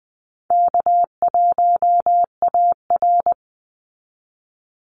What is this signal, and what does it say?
Morse code, heard as a keyed tone, about 20 wpm, 700 Hz. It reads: K1AR
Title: Easy Call Signs Repeated 2x - 20wpm